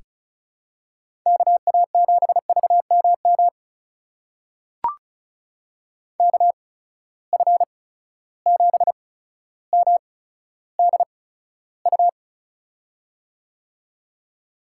Morse code, heard as a keyed tone, about 36 wpm, spaced as for 15 wpm, 700 Hz. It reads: KA7VMM KF7MDU
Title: USA Call Signs - 6 Characters - 8x Character Spacing - 35wpm